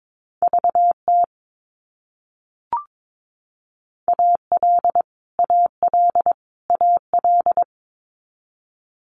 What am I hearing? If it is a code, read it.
Morse code, heard as a keyed tone, about 22 wpm, 700 Hz. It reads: VT AL AL AL
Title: US States Repeated 3x 22wpm